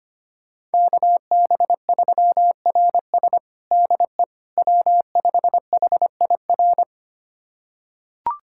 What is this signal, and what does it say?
Morse code, heard as a keyed tone, about 25 wpm, 700 Hz. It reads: KB3RS DE W5HIR
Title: USA Callsign DE Callsign 25wpm